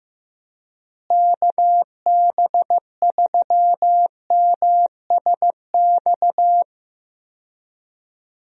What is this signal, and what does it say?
Morse code, heard as a keyed tone, about 15 wpm, 700 Hz. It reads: KB3MSX
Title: USA Call Signs - 6 Characters - Repeated 3x - 15wpm